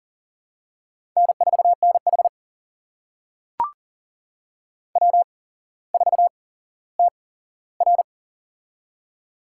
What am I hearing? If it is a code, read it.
Morse code, heard as a keyed tone, about 40 wpm, spaced as for 17 wpm, 700 Hz. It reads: N4NH W4TR
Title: USA Call Signs - 4 Characters - 8x Character Spacing - 40wpm